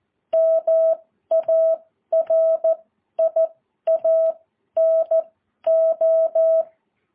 Morse code, heard as a keyed tone, about 14 wpm, spaced as for 11 wpm, 650 Hz. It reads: MARIANO
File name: jh8cbh